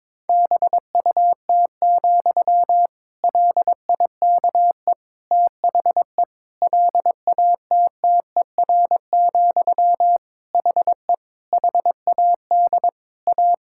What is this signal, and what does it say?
Morse code, heard as a keyed tone, about 21 wpm, 700 Hz. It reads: BUT, LIKE THE LATTER, HE HAD A
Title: Sentences from Top 1600 Words 22wpm